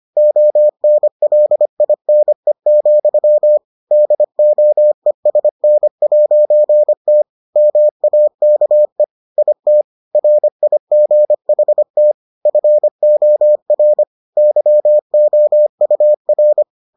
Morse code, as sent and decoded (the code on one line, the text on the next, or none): --- -. .-.. .. -. --..-- / -.. --- . ... -. .----. - / -- .- -.- . / .. - / .-. .. --. .... - / ..-. --- .-. / -.-- --- ..- .-.
ONLINE, DOESN'T MAKE IT RIGHT FOR YOUR